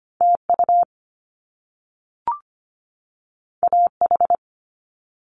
Morse code, heard as a keyed tone, about 25 wpm, 700 Hz. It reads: TU AH
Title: Top Two Letter Combinations in Words 25wpm